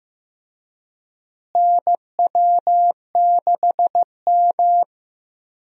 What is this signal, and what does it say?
Morse code, heard as a keyed tone, about 15 wpm, 700 Hz. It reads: NW6M
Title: USA Call Signs - 4 Characters - Repeated 3x - 15wpm